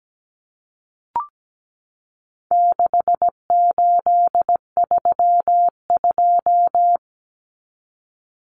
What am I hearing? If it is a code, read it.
Morse code, heard as a keyed tone, about 17 wpm, 700 Hz. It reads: 6832
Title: Numbers - 4-Digit 17wpm